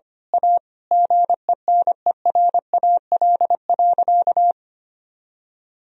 Morse code, heard as a keyed tone, about 25 wpm, 700 Hz. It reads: A GENERAL.
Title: Sentences from Top 1000 Words - Review Part 3 - 25wpm